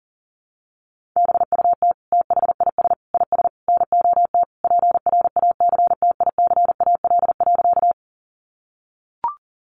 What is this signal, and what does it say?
Morse code, heard as a keyed tone, about 40 wpm, 700 Hz. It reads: BUT THIS IS NOT PRACTICAL.
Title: Sentences from Top 1400 Words 40wpm